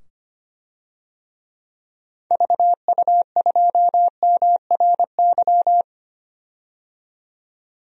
Morse code, heard as a keyed tone, about 25 wpm, 700 Hz. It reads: VU2MRY
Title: DXCC-100 Call Signs - 6 Characters - 25wpm